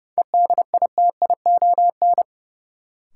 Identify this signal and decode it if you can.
Morse code, heard as a keyed tone, about 30 wpm, 700 Hz. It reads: EDITION
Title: Sets of 5 Words - New Word Drill - 30wpm